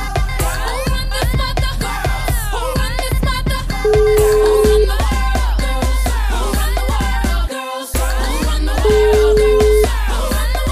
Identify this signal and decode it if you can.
Ringback tone (Europe, 425 Hz), with noise.